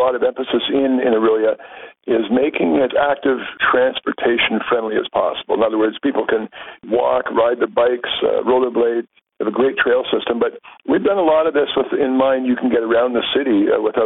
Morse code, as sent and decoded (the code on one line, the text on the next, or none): none